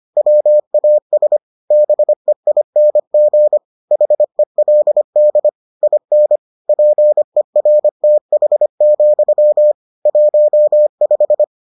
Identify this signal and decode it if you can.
Morse code, heard as a keyed tone, about 25 wpm, 600 Hz. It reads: WAS BEING HELD IN PERTH, 15